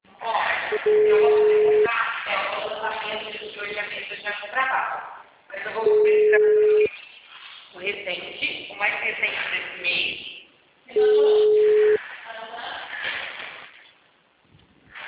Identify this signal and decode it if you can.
Ringback tone (Europe, 425 Hz), with speech.
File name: out- -2043- - - .13820